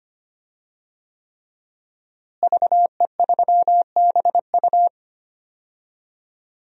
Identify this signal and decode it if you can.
Morse code, heard as a keyed tone, about 25 wpm, 700 Hz. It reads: VE3BU
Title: DXCC-100 Call Signs - 5 Characters - Repeated 3x - 25wpm